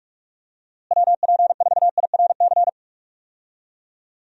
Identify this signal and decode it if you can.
Morse code, heard as a keyed tone, about 43 wpm, 700 Hz. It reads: WP4IRC
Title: USA Call Signs - 6 Characters - Repeated 3x - 45wpm